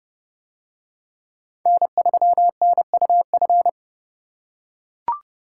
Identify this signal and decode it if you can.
Morse code, heard as a keyed tone, about 30 wpm, 700 Hz. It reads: N3NUF